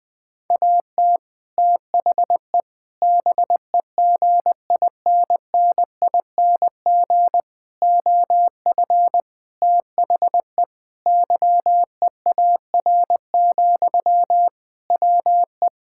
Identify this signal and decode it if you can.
Morse code, heard as a keyed tone, about 20 wpm, 700 Hz. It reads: AT THE BEGINNING OF THE YEAR, WE
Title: Sentences from Top 1000 Words - Review Part 5 - 20wpm